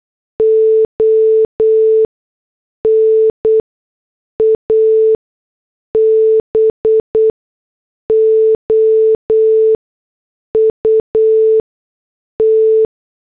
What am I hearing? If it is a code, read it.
Morse code, heard as a keyed tone, about 8 wpm, spaced as for 6 wpm, 440 Hz. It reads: ON ABOUT
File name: morse / headlines